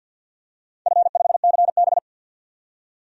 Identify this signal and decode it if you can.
Morse code, heard as a keyed tone, about 53 wpm, 700 Hz. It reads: F5CB
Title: DXCC-100 Call Signs - 4 Characters - 50wpm